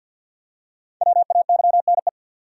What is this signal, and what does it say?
Morse code, heard as a keyed tone, about 48 wpm, 700 Hz. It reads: WAYNE
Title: USA Common Names - 50wpm